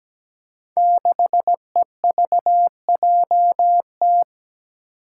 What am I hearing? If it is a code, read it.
Morse code, heard as a keyed tone, about 17 wpm, 700 Hz. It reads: 6EVJT
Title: Five Challenging Letter-Number Combinations 17wpm